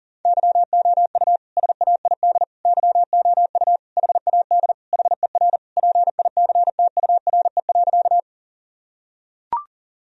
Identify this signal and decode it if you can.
Morse code, heard as a keyed tone, about 40 wpm, 700 Hz. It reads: YOU SAID YOU HAD HER PICTURE.
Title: Sentences from Top 1000 Words - Review Part 1 - 40wpm